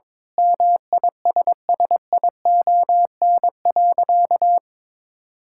Morse code, heard as a keyed tone, about 22 wpm, 700 Hz. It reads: MISSION.
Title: Sentences from Top 1600 Words 22wpm